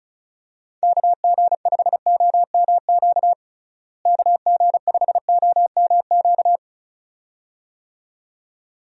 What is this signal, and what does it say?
Morse code, heard as a keyed tone, about 36 wpm, 700 Hz. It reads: KG5OMQ KG5OMQ
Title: USA Call Signs - 6 Characters - Repeated 2x - 35wpm